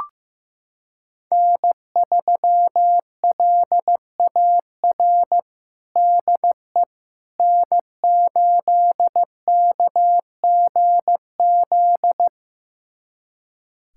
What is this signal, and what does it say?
Morse code, heard as a keyed tone, about 16 wpm, 700 Hz. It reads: N3LAR DE N8KGZ
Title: USA Callsign DE Callsign - 5 Characters - 15wpm